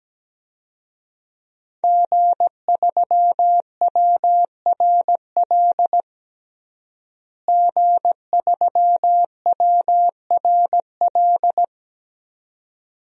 Morse code, heard as a keyed tone, about 17 wpm, 700 Hz. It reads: G3WRL G3WRL
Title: DXCC-100 Call Signs - 5 Characters - Repeated 2x - 17wpm